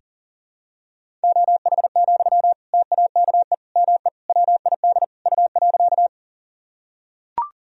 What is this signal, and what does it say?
Morse code, heard as a keyed tone, about 40 wpm, 700 Hz. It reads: OH, TAKE ME WID U.
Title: Sentences from Top 500 Words and Abbreviations 40wpm